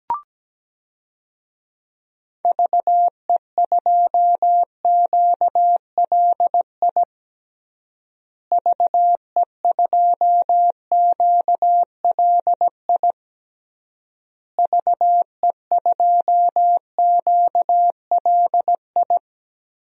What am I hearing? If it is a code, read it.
Morse code, heard as a keyed tone, about 17 wpm, 700 Hz. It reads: VE2QLI VE2QLI VE2QLI
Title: DXCC-100 Call Signs - 6 Characters - Repeated 3x - 17wpm